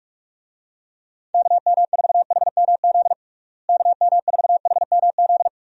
Morse code, heard as a keyed tone, about 41 wpm, 700 Hz. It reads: KM4HMZ KM4HMZ
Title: USA Call Signs - 6 Characters - Repeated 3x - 45wpm